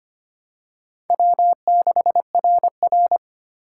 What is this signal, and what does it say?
Morse code, heard as a keyed tone, about 25 wpm, 700 Hz. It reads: W6RR
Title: USA Call Signs - 4 Characters - 3x Character Spacing - 25wpm